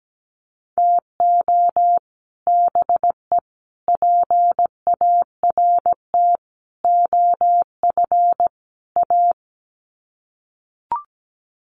Morse code, heard as a keyed tone, about 17 wpm, 700 Hz. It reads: TO BE PART OF A
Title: Sets of 5 Words Encore2 17wpm